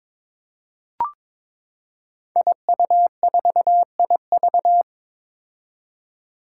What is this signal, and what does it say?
Morse code, heard as a keyed tone, about 22 wpm, 700 Hz. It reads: IU4IV
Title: Five Challenging Letter-Number Combinations 22wpm